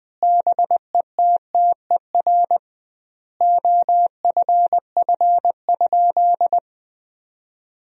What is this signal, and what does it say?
Morse code, heard as a keyed tone, about 21 wpm, 700 Hz. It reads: BETTER OFF?
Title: 2x Word-Spacing - Sentences from Top 500 Words - 20wpm